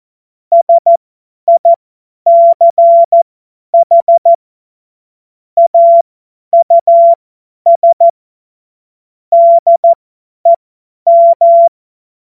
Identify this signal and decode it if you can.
Morse code, heard as a keyed tone, about 14 wpm, spaced as for 10 wpm, 680 Hz. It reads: SICH AUS DEM
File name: Morsekurs / Witze / Witz 01. März 2023 50 F70